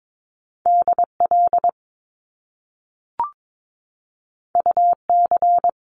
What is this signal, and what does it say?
Morse code, heard as a keyed tone, about 22 wpm, 700 Hz. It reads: DL UC
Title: Top Two Letter Combinations in Words 22wpm